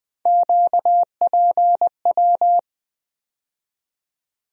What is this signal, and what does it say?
Morse code, heard as a keyed tone, about 21 wpm, 700 Hz. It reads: QPW
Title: USA Call Sign Postfixes - 3 Letters 20wpm